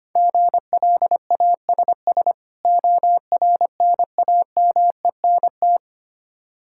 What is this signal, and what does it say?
Morse code, heard as a keyed tone, about 25 wpm, 700 Hz. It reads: GLASS ORNAMENT
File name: practice / 25wpm / Difficult Words with Hints - 25wpm